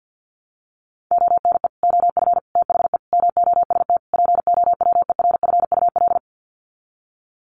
Morse code, heard as a keyed tone, about 50 wpm, 700 Hz. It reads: ONE OF THE MOST POWERFUL
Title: Sets of 5 Words Encore 50wpm